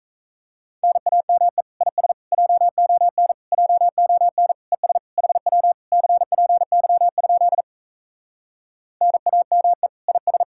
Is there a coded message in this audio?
Yes: Morse code from 0.8 to 10.4 s, heard as a keyed tone, about 38 wpm, 700 Hz. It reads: NAME IS JON JON ES HW CPY? NAME IS